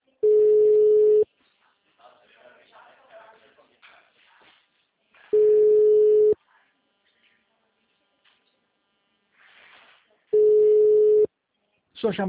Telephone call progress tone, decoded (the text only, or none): ringback tone